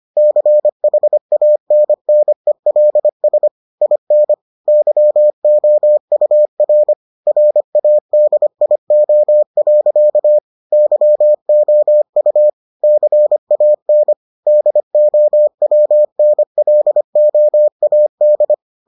Morse code, heard as a keyed tone, about 25 wpm, 600 Hz. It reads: CHANNELS IN YOUR RADIO. YOU CAN DOWNLOAD